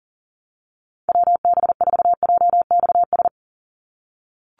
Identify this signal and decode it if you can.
Morse code, heard as a keyed tone, about 39 wpm, 700 Hz. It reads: WB4JXS
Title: USA Call Signs 40wpm